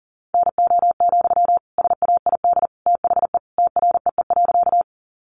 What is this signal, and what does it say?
Morse code, heard as a keyed tone, about 40 wpm, 700 Hz. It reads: NO, SAID THE TREE.
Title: Sentences from Top 300 Words 40wpm